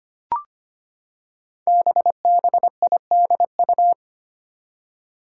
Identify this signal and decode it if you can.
Morse code, heard as a keyed tone, about 25 wpm, 700 Hz. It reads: BBIDU
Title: Five Challenging Letter-Number Combinations 25wpm